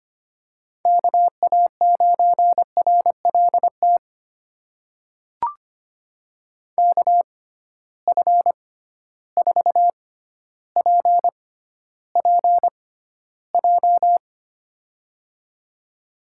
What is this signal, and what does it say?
Morse code, heard as a keyed tone, about 25 wpm, spaced as for 13 wpm, 700 Hz. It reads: KA9RLT KF4PPJ
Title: USA Call Signs - 6x Character Spacing - 25wpm